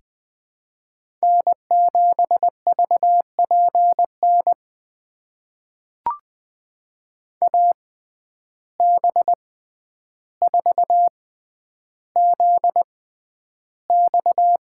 Morse code, heard as a keyed tone, about 20 wpm, spaced as for 10 wpm, 700 Hz. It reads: N7VPN AB4ZX
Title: USA Call Signs - 5 Characters - 6x Character Spacing - 20wpm